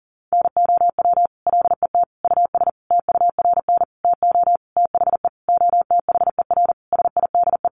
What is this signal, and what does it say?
Morse code, heard as a keyed tone, about 40 wpm, 700 Hz. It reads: NOW LET US TURN TO THE OTHER SIDE
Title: Sentences from Top 1400 Words 40wpm